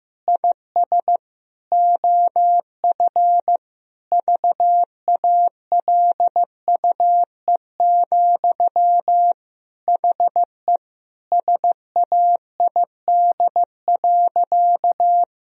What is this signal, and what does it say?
Morse code, heard as a keyed tone, about 15 wpm, 700 Hz. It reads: IS OF VALUE, HE SAID.
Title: Sentences from Top 1000 Words - Review Part 3 - 15wpm